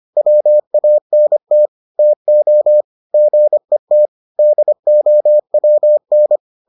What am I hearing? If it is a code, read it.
Morse code, heard as a keyed tone, about 25 wpm, 600 Hz. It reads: WANT TO GET DOWN